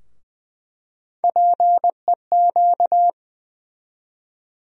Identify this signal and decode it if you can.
Morse code, heard as a keyed tone, about 20 wpm, 700 Hz. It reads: PEQ
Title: USA Call Sign Postfixes - 3 Letters 20wpm